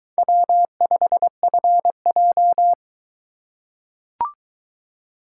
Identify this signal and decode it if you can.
Morse code, heard as a keyed tone, about 22 wpm, 700 Hz. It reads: W5FJ